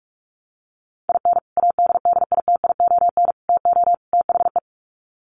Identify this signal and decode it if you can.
Morse code, heard as a keyed tone, about 44 wpm, 700 Hz. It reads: IN ADDITION TO THE
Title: Sets of 4 Words 45wpm